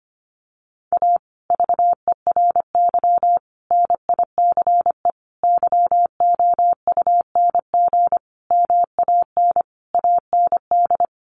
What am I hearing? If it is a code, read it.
Morse code, heard as a keyed tone, about 25 wpm, 700 Hz. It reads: A VERY NICE YOUNG MAN AND
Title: Sentences from Top 1200 Words 25wpm